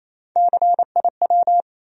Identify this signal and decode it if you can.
Morse code, heard as a keyed tone, about 28 wpm, 700 Hz. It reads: CIW
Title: USA Call Sign Postfixes - 3 Letters 28wpm